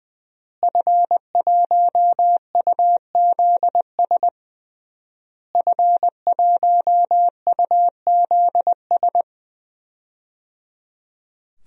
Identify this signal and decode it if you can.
Morse code, heard as a keyed tone, about 20 wpm, 700 Hz. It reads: F1UZS F1UZS
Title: DXCC-100 Call Signs - 5 Characters - Repeated 3x - 20wpm